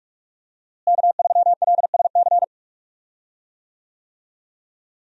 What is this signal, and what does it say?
Morse code, heard as a keyed tone, about 47 wpm, 700 Hz. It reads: K3LSC
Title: USA Call Signs - 5 Characters - Repeated 3x - 45wpm